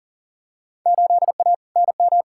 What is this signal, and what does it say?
Morse code, heard as a keyed tone, about 40 wpm, 700 Hz. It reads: 8A NM
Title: Field Day Exchanges Encore - 40wpm